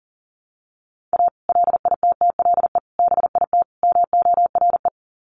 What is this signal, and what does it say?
Morse code, heard as a keyed tone, about 40 wpm, 700 Hz. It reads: A LITTLE BIT MORE
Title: Sets of 4 Words 40wpm